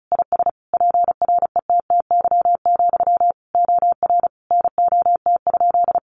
Morse code, heard as a keyed tone, about 34 wpm, 700 Hz. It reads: IS PRETTY, OR NOT?
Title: Sentences from Top 1000 Words 35wpm